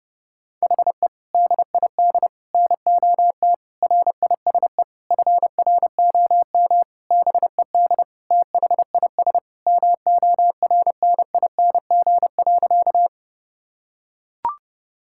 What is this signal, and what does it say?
Morse code, heard as a keyed tone, about 30 wpm, 700 Hz. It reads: HE DID NOT RISE FROM BED THIS MORNING.
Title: Sentences from Top 1000 Words - Review Part 6 - 30wpm